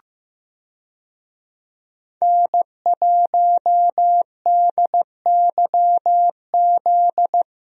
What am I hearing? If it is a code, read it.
Morse code, heard as a keyed tone, about 15 wpm, 700 Hz. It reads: N1DYZ
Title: USA Call Signs - 5 Characters - Repeated 3x - 15wpm